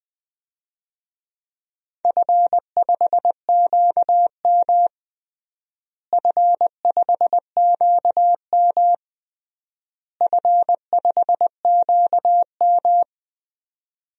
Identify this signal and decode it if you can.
Morse code, heard as a keyed tone, about 21 wpm, 700 Hz. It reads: F5QM F5QM F5QM
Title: DXCC-100 Call Signs - 4 Characters - Repeated 3x - 20wpm